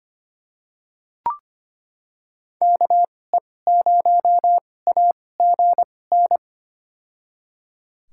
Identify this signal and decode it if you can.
Morse code, heard as a keyed tone, about 25 wpm, spaced as for 18 wpm, 700 Hz. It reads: KE0AGN
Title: USA Call Signs - 2x Character Spacing - 25wpm